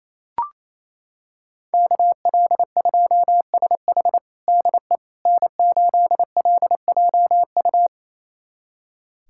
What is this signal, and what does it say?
Morse code, heard as a keyed tone, about 28 wpm, 700 Hz. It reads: KL2SH DE N8LJU
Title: USA Callsign DE Callsign 28wpm